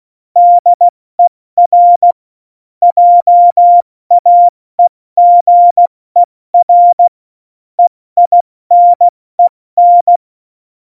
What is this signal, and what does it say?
Morse code, heard as a keyed tone, about 16 wpm, 700 Hz. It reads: DER JAEGER EINEN
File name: Morsekurs / Witze / Witz 13. Oktober 2020, 70 F80